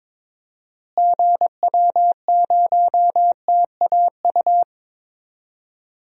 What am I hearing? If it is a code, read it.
Morse code, heard as a keyed tone, about 22 wpm, 700 Hz. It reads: GW0TAU
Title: DXCC-100 Call Signs - 6 Characters - Repeated 3x - 22wpm